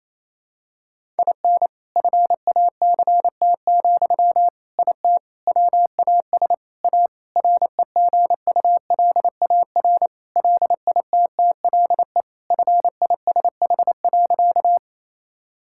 Morse code, heard as a keyed tone, about 28 wpm, 700 Hz. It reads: IN FACT, IT WAS A REGULAR LITTLE FISH.
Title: Sentences from Top 1000 Words - Review Part 6 - 28wpm